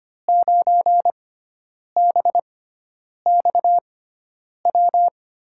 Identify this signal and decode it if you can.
Morse code, heard as a keyed tone, about 25 wpm, spaced as for 9 wpm, 700 Hz. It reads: 9BXW
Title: USA Call Signs - 6 Characters - 6x Character Spacing - 25wpm